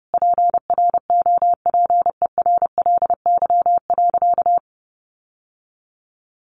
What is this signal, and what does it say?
Morse code, heard as a keyed tone, about 30 wpm, 700 Hz. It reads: PROPERLY.
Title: Sentences from Top 1100 Words 30wpm